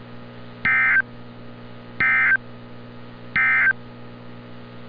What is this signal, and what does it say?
Emergency Alert System (SAME) tones, heard as harsh buzzing bursts. It is heard from 0.5 to 3.8 s.
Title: NOAA Weather Radio - Altoona, PA